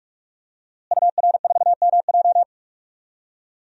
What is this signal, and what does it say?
Morse code, heard as a keyed tone, about 44 wpm, 700 Hz. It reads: UR4MJ
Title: DXCC-100 Call Signs - 5 Characters - Repeated 2x - 45wpm